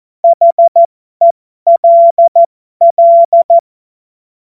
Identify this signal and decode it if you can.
Morse code, heard as a keyed tone, about 14 wpm, spaced as for 12 wpm, 680 Hz. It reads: HELL
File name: Morsekurs / Witze